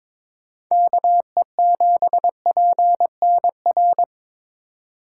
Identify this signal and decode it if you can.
Morse code, heard as a keyed tone, about 25 wpm, 700 Hz. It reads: KE7PNR
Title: Speed Racing - USA 2x3 Call Signs 22wpm